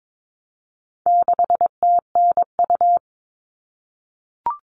Morse code, heard as a keyed tone, about 22 wpm, 700 Hz. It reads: 6TNU
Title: All Cut Numbers - 4-Digit 22wpm